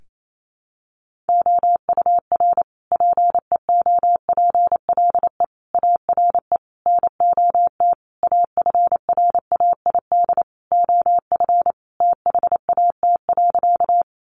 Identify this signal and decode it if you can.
Morse code, heard as a keyed tone, about 28 wpm, 700 Hz. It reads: OUR PEOPLE ARE NOT AFRAID OF THAT.
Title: Sentences from Top 1500 Words 28wpm